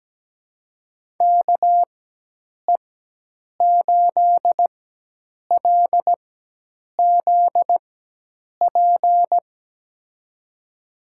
Morse code, heard as a keyed tone, about 17 wpm, spaced as for 8 wpm, 700 Hz. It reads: KE8LZP
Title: USA Call Signs - 6 Characters - 4x Character Spacing - 17wpm